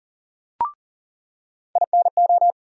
Morse code, heard as a keyed tone, about 40 wpm, 700 Hz. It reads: INO
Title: USA Call Sign Postfixes - 3 Letters 40wpm